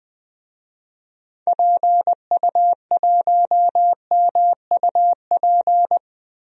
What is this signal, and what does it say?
Morse code, heard as a keyed tone, about 20 wpm, 700 Hz. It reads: PU1MUP
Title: DXCC-100 Call Signs - 6 Characters - Repeated 2x - 20wpm